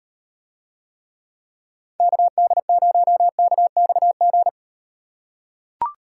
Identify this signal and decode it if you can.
Morse code, heard as a keyed tone, about 38 wpm, 700 Hz. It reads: KD0KXG